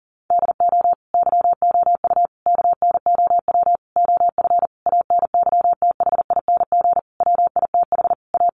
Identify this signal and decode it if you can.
Morse code, heard as a keyed tone, about 40 wpm, 700 Hz. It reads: DO YOU KNOW OF ANYTHING WITH A
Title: Sentences from Top 1200 Words 40wpm